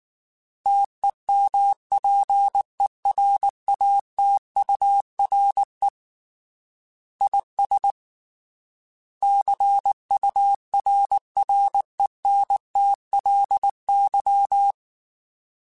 Morse code, heard as a keyed tone, about 19 wpm, 800 Hz. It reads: TEMPERATURE IS CURRENTLY
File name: Weather Sets / Weather Sets-mp3s / Weather 4 28 @18